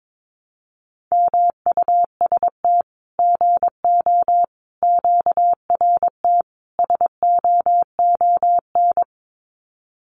Morse code, heard as a keyed tone, about 22 wpm, 700 Hz. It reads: MUST GO QRT SOON
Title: Common QSO Phrases 22wpm